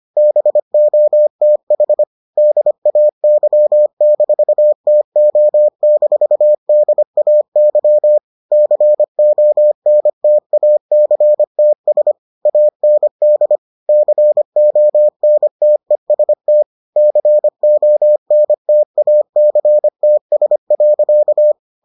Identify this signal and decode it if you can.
Morse code, heard as a keyed tone, about 25 wpm, 600 Hz. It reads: BOTH DAY-TO-DAY CONTACTS AND CONTEST CONTACTS.